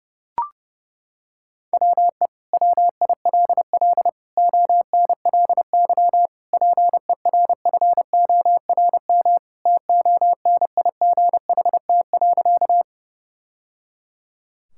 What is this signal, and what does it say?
Morse code, heard as a keyed tone, about 32 wpm, 700 Hz. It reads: WE WILL ONLY PERFORM TONIGHT.
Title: Sentences from Top 1000 Words - Review Part 5 - 30wpm